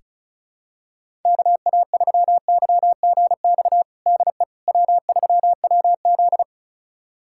Morse code, heard as a keyed tone, about 36 wpm, 700 Hz. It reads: KA3YGX DE W3WZ
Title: USA Callsign DE Callsign 35wpm